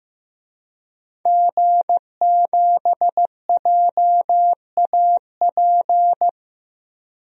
Morse code, heard as a keyed tone, about 15 wpm, 700 Hz. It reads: G7JAP
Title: DXCC-100 Call Signs - 5 Characters - 15wpm